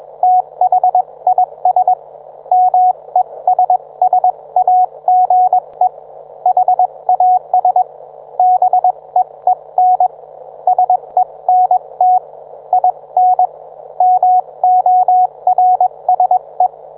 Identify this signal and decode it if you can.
Morse code, heard as a keyed tone, about 21 wpm, spaced as for 19 wpm, 730 Hz. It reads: THIS MESSAGE HAS BEEN SENT IN MORSE
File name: book